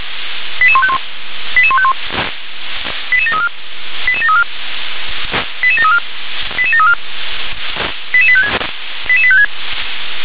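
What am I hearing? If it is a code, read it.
Five-tone selective call (ZVEI), heard as quick runs of five tones, tones of about 70 ms. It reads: 88151 88151 88734 88734 88734 88734 88856 88856
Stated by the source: SDR-received ZVEI five-tone selective call.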